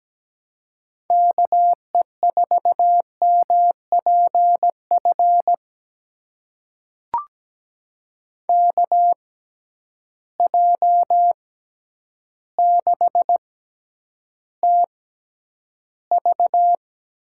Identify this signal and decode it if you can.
Morse code, heard as a keyed tone, about 17 wpm, spaced as for 9 wpm, 700 Hz. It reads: KE4MPF KJ6TV
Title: USA Call Signs - 6 Characters - 6x Character Spacing - 17wpm